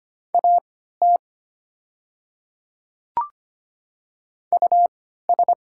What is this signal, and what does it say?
Morse code, heard as a keyed tone, about 25 wpm, spaced as for 14 wpm, 690 Hz. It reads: AT US
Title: 3x Character Spacing - Top 300 Words - 25wpm